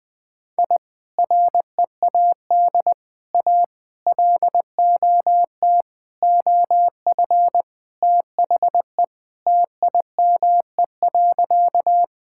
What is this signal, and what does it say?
Morse code, heard as a keyed tone, about 21 wpm, 700 Hz. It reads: I READ A LOT OF THE TIME.
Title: Sentences with 4-Word Phrases - 20wpm